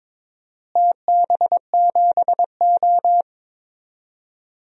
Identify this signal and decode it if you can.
Morse code, heard as a keyed tone, about 21 wpm, 700 Hz. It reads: TB7O
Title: DXCC-100 Call Signs - 4 Characters - Repeated 2x - 22wpm